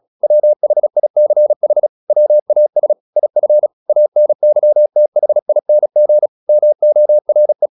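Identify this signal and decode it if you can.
Morse code, heard as a keyed tone, about 36 wpm, 600 Hz. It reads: WHICH WAS IF ANYTHING MORE
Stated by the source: Morse practice at 36 wpm, text of Ken Follett's novel Never, from CW med Gnister.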